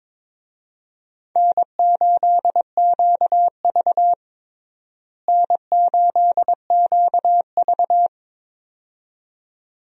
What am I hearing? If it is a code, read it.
Morse code, heard as a keyed tone, about 22 wpm, 700 Hz. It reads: N8QV N8QV
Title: USA Call Signs - 4 Characters - Repeated 2x - 22wpm